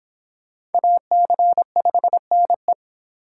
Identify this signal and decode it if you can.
Morse code, heard as a keyed tone, about 26 wpm, 700 Hz. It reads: AC5NE